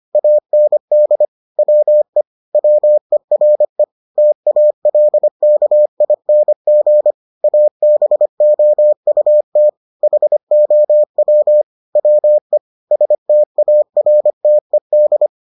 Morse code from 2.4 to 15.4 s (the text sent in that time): WERE TALKING ABOUT HOW WE STARTED